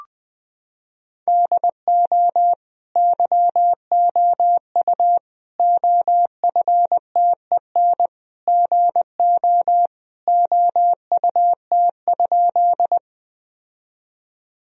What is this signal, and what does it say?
Morse code, heard as a keyed tone, about 19 wpm, 700 Hz. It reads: DO YOU OFTEN GO OUT?
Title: Sentences from Top 1000 Words - Review Part 1 - 20wpm